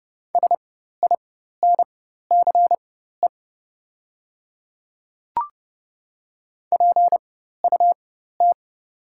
Morse code, heard as a keyed tone, about 30 wpm, spaced as for 14 wpm, 700 Hz. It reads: SINCE PUT
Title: 4x Character Spacing - Top 500 Words - 30wpm